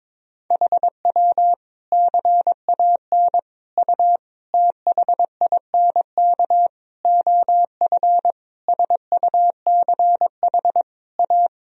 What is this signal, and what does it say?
Morse code, heard as a keyed tone, about 22 wpm, 700 Hz. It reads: HW CAN U THINK OF SUCH A
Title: Sentences from Top 200 Words and Abbreviations 22wpm